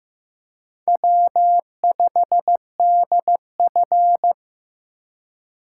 Morse code, heard as a keyed tone, about 15 wpm, 700 Hz. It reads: W5DF
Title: USA Call Signs - 4 Characters - 3x Character Spacing - 15wpm